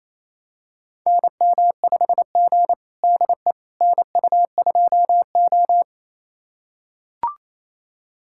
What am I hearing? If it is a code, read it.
Morse code, heard as a keyed tone, about 27 wpm, 700 Hz. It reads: NM5G DE NU2O
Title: USA Callsign DE Callsign - 4 Characters - 28wpm